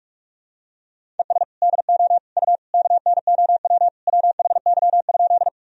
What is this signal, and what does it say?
Morse code, heard as a keyed tone, about 47 wpm, 700 Hz. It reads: ES DO U KNOW WHY?
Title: Sentences from Top 200 Words and Abbreviations 45wpm